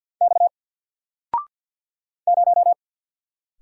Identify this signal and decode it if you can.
Morse code, heard as a keyed tone, about 48 wpm, 700 Hz. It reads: X 0
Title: Single Letter-Number Alt 50wpm